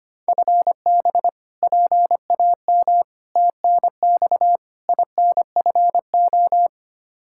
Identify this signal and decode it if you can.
Morse code, heard as a keyed tone, about 25 wpm, 700 Hz. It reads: FB PAM TNX INFO
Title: K1USN SST QSO Practice 25wpm